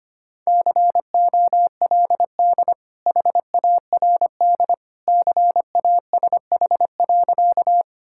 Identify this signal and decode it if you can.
Morse code, heard as a keyed tone, about 25 wpm, 700 Hz. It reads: COLD HARD CASH.
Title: Sentences from Top 1000 Words - Review Part 5 - 25wpm